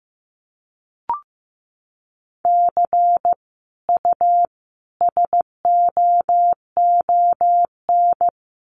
Morse code, heard as a keyed tone, about 15 wpm, 700 Hz. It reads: C U SOON
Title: Common QSO Phrases 15wpm